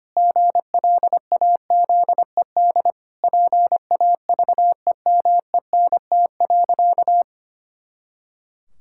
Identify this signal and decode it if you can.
Morse code, heard as a keyed tone, about 25 wpm, 700 Hz. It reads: GLAZED PAVEMENT.
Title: Pangram Challenge v2 - 25wpm